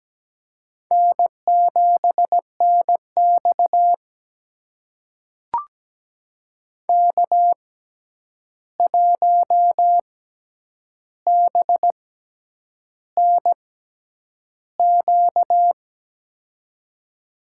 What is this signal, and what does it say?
Morse code, heard as a keyed tone, about 17 wpm, spaced as for 8 wpm, 700 Hz. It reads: N7NX K1BNQ
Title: USA Call Signs - 6x Character Spacing - 17wpm